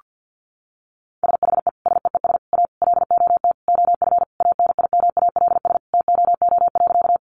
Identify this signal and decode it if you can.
Morse code, heard as a keyed tone, about 50 wpm, 700 Hz. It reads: SHE SEES A LOT OF ANIMALS TOO.
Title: Easy Phrases 50wpm